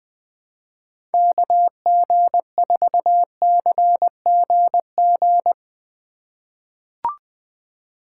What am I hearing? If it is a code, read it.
Morse code, heard as a keyed tone, about 21 wpm, 700 Hz. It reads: KG4CGG
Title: USA Call Signs - 6 Characters - 3x Character Spacing - 20wpm